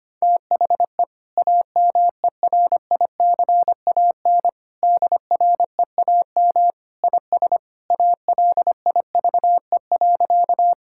Morse code, heard as a keyed tone, about 25 wpm, 690 Hz. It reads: THE AMERICAN DREAM IS ALIVE.
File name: practice / 25wpm / Sentences with 2-Word Phrases - 25wpm